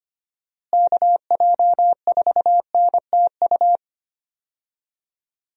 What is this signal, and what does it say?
Morse code, heard as a keyed tone, about 25 wpm, 700 Hz. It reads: KJ4NTU
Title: USA Call Signs - 6 Characters - Repeated 3x - 25wpm